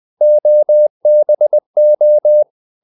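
Morse code, heard as a keyed tone, about 20 wpm, 600 Hz. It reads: OBO
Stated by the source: Morse code practice of short Danish words at 20 wpm, from CW med Gnister.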